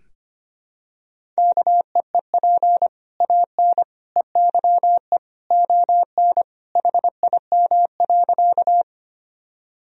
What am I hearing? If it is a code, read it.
Morse code, heard as a keyed tone, about 25 wpm, 700 Hz. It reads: KEEP AN EYE ON HIM.
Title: Sentences from Top 1000 Words - Review Part 1 - 25wpm